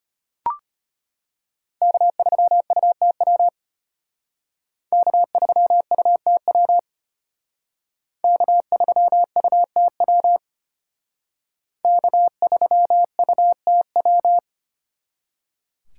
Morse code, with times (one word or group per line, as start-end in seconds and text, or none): 1.8-3.5 K3UTW
4.9-6.8 K3UTW
8.2-10.4 K3UTW
11.8-14.4 K3UTW